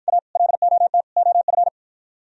Morse code, heard as a keyed tone, about 52 wpm, 700 Hz. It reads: A LOT OF